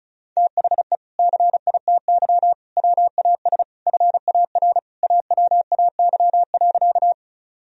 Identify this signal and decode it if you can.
Morse code, heard as a keyed tone, about 34 wpm, 700 Hz. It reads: THE CITY WAS FAR AWAY.
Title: Sentences from Top 1000 Words - Review Part 1 - 35wpm